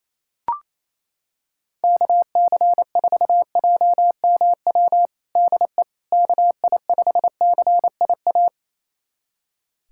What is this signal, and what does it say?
Morse code, heard as a keyed tone, about 28 wpm, 700 Hz. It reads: KC4JMW DE KI5CIA
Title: USA Callsign DE Callsign 28wpm